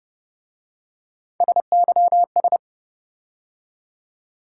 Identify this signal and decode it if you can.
Morse code, heard as a keyed tone, about 28 wpm, 700 Hz. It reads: SYS